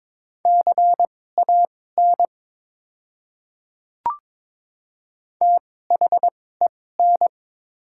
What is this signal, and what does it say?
Morse code, heard as a keyed tone, about 22 wpm, spaced as for 16 wpm, 690 Hz. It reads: CAN THEN
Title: 2x Character Spacing - Top 100 Words - 22wpm